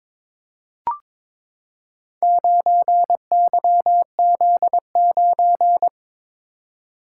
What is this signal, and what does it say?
Morse code, heard as a keyed tone, about 22 wpm, 690 Hz. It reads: 9YZ9
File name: practice / 22wpm / Four Letter-Number Combinations - 22wpm